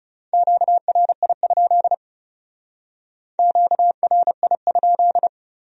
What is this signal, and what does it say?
Morse code, heard as a keyed tone, about 39 wpm, 700 Hz. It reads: QRI? QRI?